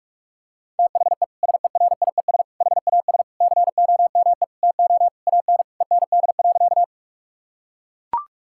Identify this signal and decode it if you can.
Morse code, heard as a keyed tone, about 44 wpm, 700 Hz. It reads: THE SERIES HAS COME TO AN END.
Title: Sentences from Top 1000 Words - Review Part 5 - 45wpm